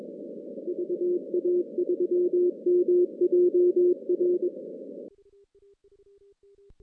Morse code, heard as a keyed tone, about 23 wpm, 360 Hz. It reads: VA3MJR
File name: 2016 JAN CW NAQP / 2016 Jan naqp k5ot - 0-10